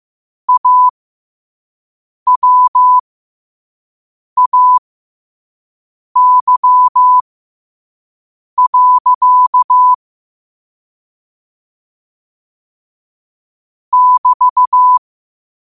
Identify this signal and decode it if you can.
Morse code, heard as a keyed tone, about 15 wpm, spaced as for 5 wpm, 1000 Hz. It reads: AWAY. <BT>
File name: qotdfiles / 5wpm